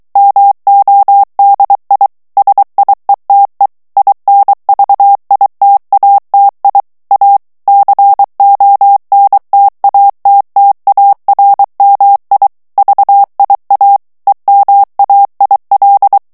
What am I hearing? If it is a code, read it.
Morse code, heard as a keyed tone, about 23 wpm, 800 Hz. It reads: MODI SIETE INVITATI A CONTATTARMI VIA EMAIL
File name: morse mp3 / ITALIEN texte clair 1 vitesse 22 mots minute